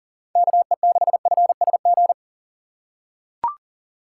Morse code, heard as a keyed tone, about 37 wpm, 700 Hz. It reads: KE6FSG
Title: USA Call Signs - 3x Character Spacing - 40wpm